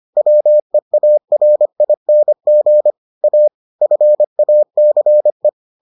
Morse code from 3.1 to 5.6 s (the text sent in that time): A FACE